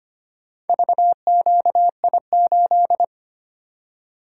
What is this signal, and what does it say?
Morse code, heard as a keyed tone, about 25 wpm, 700 Hz. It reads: VQI8
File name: practice / 25wpm / Four Letter-Number Combinations - 25wpm